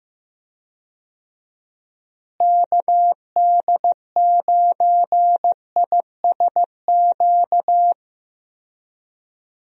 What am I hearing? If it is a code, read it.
Morse code, heard as a keyed tone, about 15 wpm, 700 Hz. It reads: KD9ISQ
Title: USA Call Signs - 6 Characters - Repeated 3x - 15wpm